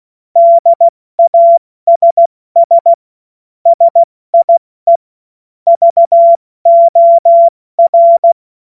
Morse code, heard as a keyed tone, about 16 wpm, spaced as for 14 wpm, 680 Hz. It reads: DASS SIE VOR
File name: Morsekurs / Witze / Witz 17. Mai 2022 70 F80